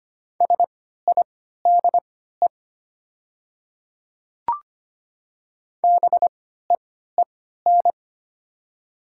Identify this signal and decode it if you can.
Morse code, heard as a keyed tone, about 25 wpm, spaced as for 14 wpm, 700 Hz. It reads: SIDE BEEN
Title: 3x Character Spacing - Top 500 Words - 25wpm